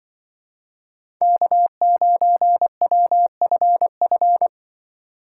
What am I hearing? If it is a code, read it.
Morse code, heard as a keyed tone, about 23 wpm, 700 Hz. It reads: K9WFF
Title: Speed Racing - USA 1x3 Call Signs 20wpm